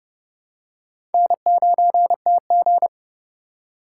Morse code, heard as a keyed tone, about 28 wpm, 700 Hz. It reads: N9TG